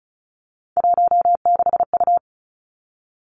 Morse code, heard as a keyed tone, about 36 wpm, 700 Hz. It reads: 16U
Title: Three Challenging Letter-Number Combinations 35wpm